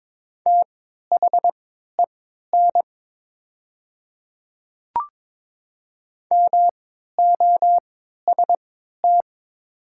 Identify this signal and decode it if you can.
Morse code, heard as a keyed tone, about 22 wpm, spaced as for 13 wpm, 690 Hz. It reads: THEN MOST
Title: 3x Character Spacing - Top 200 Words - 22wpm